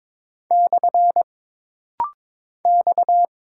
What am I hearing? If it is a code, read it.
Morse code, heard as a keyed tone, about 22 wpm, 700 Hz. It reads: / X
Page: This rapid-fire timing is geared to push your ICR (Instant Character Recognition) skills.